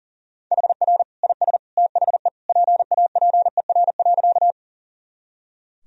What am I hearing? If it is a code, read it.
Morse code, heard as a keyed tone, about 37 wpm, 700 Hz. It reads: HR IS THE PAPER.
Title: Sentences from Top 300 Words and Abbreviations 40wpm